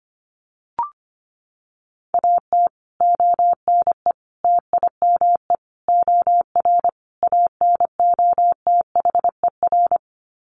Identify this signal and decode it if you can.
Morse code, heard as a keyed tone, about 25 wpm, 700 Hz. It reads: AT ONE TIME OR ANOTHER
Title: Sets of 5 Words Encore 25wpm